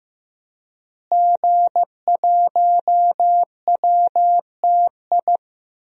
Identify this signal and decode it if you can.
Morse code, heard as a keyed tone, about 15 wpm, 700 Hz. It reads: G1WTI
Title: DXCC-100 Call Signs - 5 Characters - Repeated 2x - 15wpm